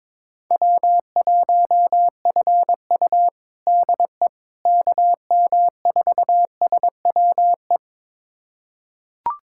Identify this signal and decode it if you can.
Morse code, heard as a keyed tone, about 22 wpm, 700 Hz. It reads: W1FU DE KM4SWE
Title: USA Callsign DE Callsign 22wpm